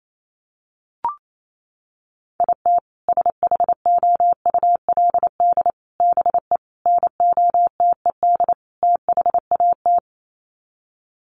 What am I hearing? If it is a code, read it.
Morse code, heard as a keyed tone, about 28 wpm, 700 Hz. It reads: IT SHOULD BE NOTED THAT
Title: Sets of 5 Words 28wpm